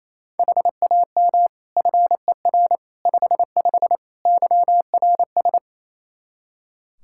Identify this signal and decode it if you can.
Morse code, heard as a keyed tone, about 28 wpm, 700 Hz. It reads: HAM FER 55 YRS
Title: Practice QSO - Exchange 3 - More Info - 28wpm